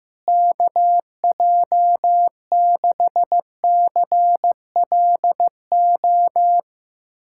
Morse code, heard as a keyed tone, about 15 wpm, 700 Hz. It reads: KJ6CLO
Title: USA Callsign DE Callsign - 6 Characters - 15wpm